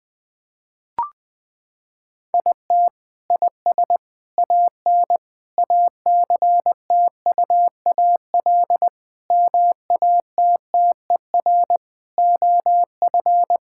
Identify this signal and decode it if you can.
Morse code, heard as a keyed tone, about 20 wpm, 700 Hz. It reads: IT IS AN ACTUAL MATTER OF
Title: Sentences from Top 1000 Words - Review Part 5 - 20wpm